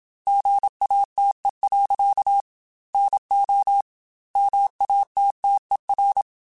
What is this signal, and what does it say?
Morse code, heard as a keyed tone, about 27 wpm, 800 Hz. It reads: GATE. NO MATTER